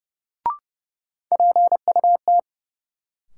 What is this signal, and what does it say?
Morse code, heard as a keyed tone, about 32 wpm, 700 Hz. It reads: PUT
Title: Sentences from Top 200 Words - New Word Drill - 30wpm